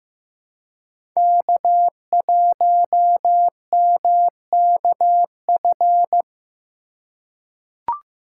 Morse code, heard as a keyed tone, about 15 wpm, 700 Hz. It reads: K1MKF
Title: USA Call Signs - 4x Character Spacing - 15wpm